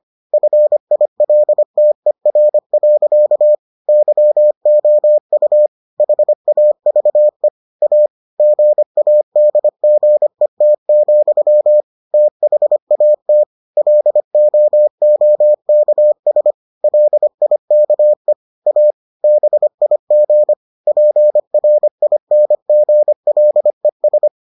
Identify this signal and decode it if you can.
Morse code, heard as a keyed tone, about 25 wpm, 600 Hz. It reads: FILTER. YOU HAVE A GADGET, THAT LOOKS LIKE A BIG PRINGLES